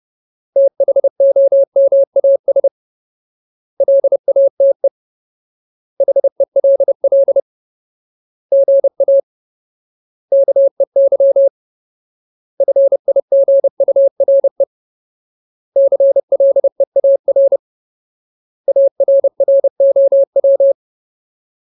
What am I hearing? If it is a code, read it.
Morse code, heard as a keyed tone, about 30 wpm, 550 Hz. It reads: THOMAS LATE HELL GA KEY FIGURE CLEAR ARROW
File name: Ultimate Word List / 30WPM Ultimate